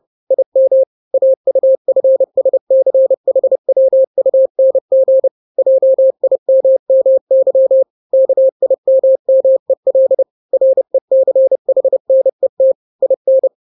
Morse code, heard as a keyed tone, about 30 wpm, 530 Hz. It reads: IM AUFSCHWUNG JIMMY KIMMEL RECHNET IN
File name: daily / 29wpm